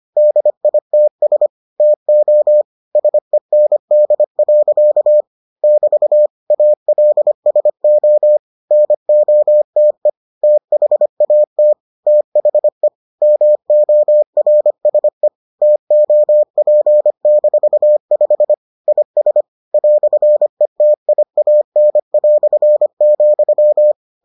Morse code, heard as a keyed tone, about 25 wpm, 600 Hz. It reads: DITS TO SEND. <BT> ALSO NOTE THAT THE MORSE TOP-5 IS "ETIAN",